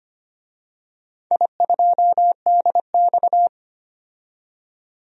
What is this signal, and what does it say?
Morse code, heard as a keyed tone, about 25 wpm, 700 Hz. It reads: I2DX
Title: DXCC-100 Call Signs - 4 Characters - 25wpm